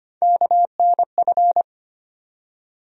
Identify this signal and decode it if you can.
Morse code, heard as a keyed tone, about 25 wpm, 700 Hz. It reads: KNF
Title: USA Call Sign Postfixes - 3 Letters 25wpm